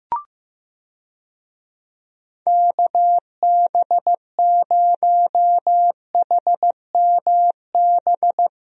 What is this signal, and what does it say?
Morse code, heard as a keyed tone, about 15 wpm, 700 Hz. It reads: KB0HMB
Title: USA Call Signs - 6 Characters - Repeated 2x - 15wpm